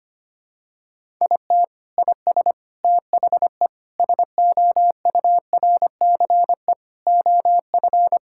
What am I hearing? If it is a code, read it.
Morse code, heard as a keyed tone, about 25 wpm, 700 Hz. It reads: IT IS THE SOURCE OF
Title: Sentences from Top 1000 Words - Review Part 5 - 25wpm